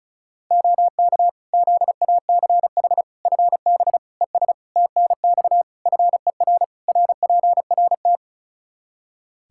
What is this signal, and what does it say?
Morse code, heard as a keyed tone, about 36 wpm, 700 Hz. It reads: OK ZACH FB ES TNX FER RPRT
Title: Practice QSO - Exchange 2 - Info - 35wpm